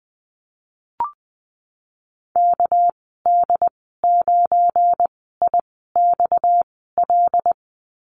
Morse code, heard as a keyed tone, about 20 wpm, spaced as for 15 wpm, 700 Hz. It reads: KD9IXL
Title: 231 Lesson 37g - Letter x - Call Signs 20wpm